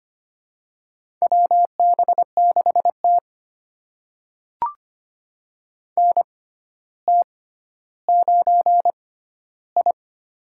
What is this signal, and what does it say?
Morse code, heard as a keyed tone, about 25 wpm, spaced as for 13 wpm, 700 Hz. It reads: WB6T NT9I
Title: USA Call Signs - 6x Character Spacing - 25wpm